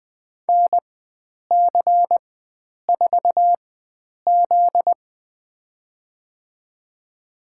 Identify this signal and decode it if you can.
Morse code, heard as a keyed tone, about 21 wpm, spaced as for 9 wpm, 700 Hz. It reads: NC4Z
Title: USA Call Signs - 4 Characters - 4x Character Spacing - 20wpm